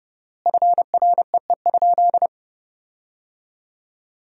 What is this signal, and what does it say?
Morse code, heard as a keyed tone, about 30 wpm, 700 Hz. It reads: FREE?
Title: 4x Word-Spacing - Sentences from Top 500 Words - 30wpm